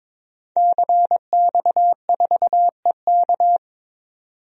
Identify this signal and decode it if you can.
Morse code, heard as a keyed tone, about 22 wpm, 700 Hz. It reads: CX4EK
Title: DXCC-100 Call Signs - 5 Characters - 22wpm